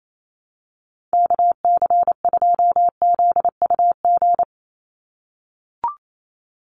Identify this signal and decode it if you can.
Morse code, heard as a keyed tone, about 28 wpm, 700 Hz. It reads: KC2ZUG
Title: USA Call Signs 28wpm